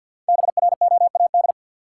Morse code, heard as a keyed tone, about 53 wpm, 700 Hz. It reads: BROAD
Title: Sentences from Top 1000 Words - New Word Drill - 50wpm